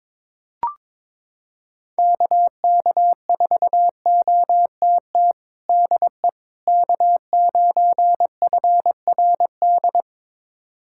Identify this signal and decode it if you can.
Morse code, heard as a keyed tone, about 22 wpm, 700 Hz. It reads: KK4OTT DE K9FRD
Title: USA Callsign DE Callsign 22wpm